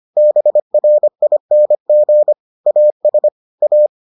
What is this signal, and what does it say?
Morse code, heard as a keyed tone, about 25 wpm, 600 Hz. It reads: BRING AS A